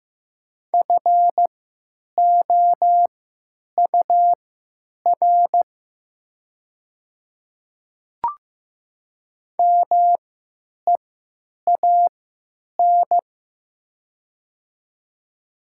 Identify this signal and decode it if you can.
Morse code, heard as a keyed tone, about 15 wpm, spaced as for 9 wpm, 700 Hz. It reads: FOUR MEAN
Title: 3x Character Spacing - Top 300 Words - 15wpm